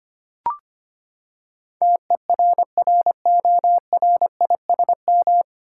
Morse code, heard as a keyed tone, about 25 wpm, 700 Hz. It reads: TERRORISM
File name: Nine Letter Words - 25wpm